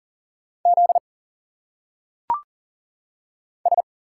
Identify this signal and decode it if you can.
Morse code, heard as a keyed tone, about 40 wpm, 700 Hz. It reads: Z S
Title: Single Letters - Phonetic Pronunciation 40wpm